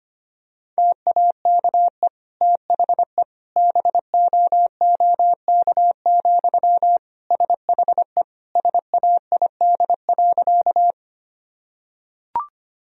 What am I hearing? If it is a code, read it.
Morse code, heard as a keyed tone, about 25 wpm, 700 Hz. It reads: TAKE THE BOOK, SHE SAID.
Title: Sentences from Top 1000 Words - Review Part 1 - 25wpm